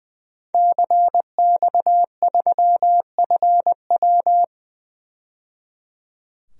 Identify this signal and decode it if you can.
Morse code, heard as a keyed tone, about 20 wpm, 700 Hz. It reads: CX3FW
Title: DXCC-100 Call Signs - 5 Characters - 20wpm